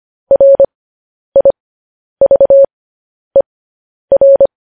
Morse code, heard as a keyed tone, about 25 wpm, spaced as for 10 wpm, 550 Hz. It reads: RIVER
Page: news headlines in morse code at 10 wpm